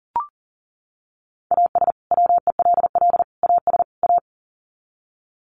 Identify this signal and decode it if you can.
Morse code, heard as a keyed tone, about 40 wpm, 700 Hz. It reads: AS WELL AS A
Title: Sets of 4 Words 40wpm